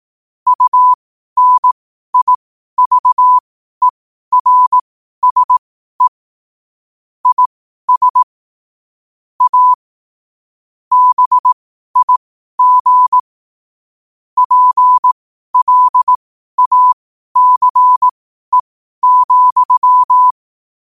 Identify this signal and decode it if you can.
Morse code, heard as a keyed tone, about 18 wpm, spaced as for 12 wpm, 1000 Hz. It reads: UNIVERSE IS A BIG PLACE,
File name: qotdfiles / 12wpm